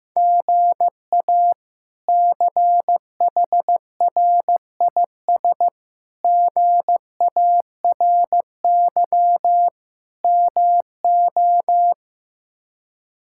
Morse code, heard as a keyed tone, about 16 wpm, 700 Hz. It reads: GA CHRIS GARY MO
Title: K1USN SST QSO Practice - 15wpm